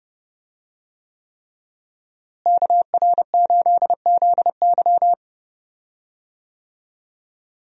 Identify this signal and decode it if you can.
Morse code, heard as a keyed tone, about 30 wpm, 700 Hz. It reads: KR8ZY
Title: USA Call Signs - 5 Characters - Repeated 3x - 30wpm